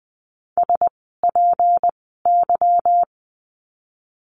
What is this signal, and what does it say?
Morse code, heard as a keyed tone, about 20 wpm, spaced as for 15 wpm, 700 Hz. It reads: SPY
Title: 153 Lesson 25f - Letter p - Words 20wpm